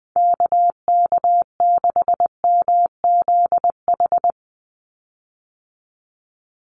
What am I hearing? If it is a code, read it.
Morse code, heard as a keyed tone, about 20 wpm, 700 Hz. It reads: KK6MZH
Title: USA Call Signs 20wpm